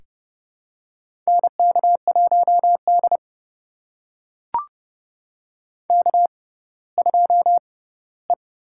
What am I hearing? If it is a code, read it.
Morse code, heard as a keyed tone, about 30 wpm, spaced as for 17 wpm, 700 Hz. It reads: NK1D K2E
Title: USA Call Signs - 4 Characters - 6x Character Spacing - 30wpm